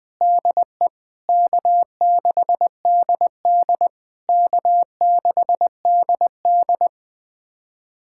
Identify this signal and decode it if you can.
Morse code, heard as a keyed tone, about 21 wpm, 700 Hz. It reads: DE K6DD K6DD
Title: SOTA QSO Practice 20wpm